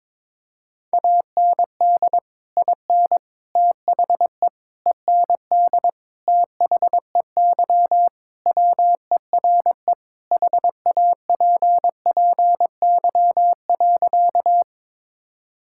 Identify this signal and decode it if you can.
Morse code, heard as a keyed tone, about 22 wpm, 700 Hz. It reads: AND IN THE END THEY WERE HAPPY.
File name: practice / 22wpm / Sentences with 2-Word Phrases - 22wpm